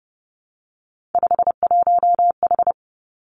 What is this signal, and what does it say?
Morse code, heard as a keyed tone, about 30 wpm, 700 Hz. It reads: 51H
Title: Three Challenging Letter-Number Combinations 30wpm